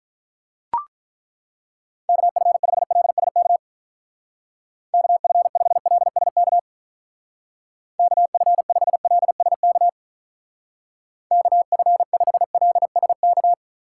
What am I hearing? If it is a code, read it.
Morse code, heard as a keyed tone, about 42 wpm, 700 Hz. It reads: KF5LSK KF5LSK KF5LSK KF5LSK
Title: Speed Racing - USA 2x3 Call Signs 35wpm